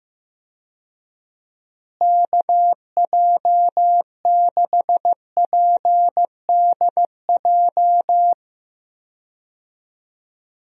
Morse code, heard as a keyed tone, about 15 wpm, 700 Hz. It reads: KJ6PDJ
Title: USA Call Signs - 6 Characters - Repeated 2x - 15wpm